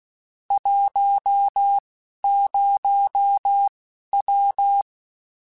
Morse code, heard as a keyed tone, about 16 wpm, spaced as for 12 wpm, 800 Hz. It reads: 10W
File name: (spaced) / QSOs Spaced-mp3s / kc7sac @15 sp